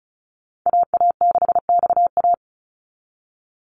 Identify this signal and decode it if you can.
Morse code, heard as a keyed tone, about 36 wpm, 700 Hz. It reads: AA6XA
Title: NA SOTA Activator Callsigns 35wpm